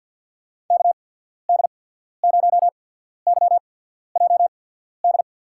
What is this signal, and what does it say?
Morse code, heard as a keyed tone, about 53 wpm, spaced as for 14 wpm, 700 Hz. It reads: KD0YJD
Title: USA Call Signs - 6 Characters - 8x Character Spacing - 50wpm